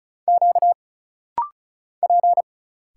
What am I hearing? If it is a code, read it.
Morse code, heard as a keyed tone, about 34 wpm, 700 Hz. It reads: Q P
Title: Single Letters - Rapid-Fire 35wpm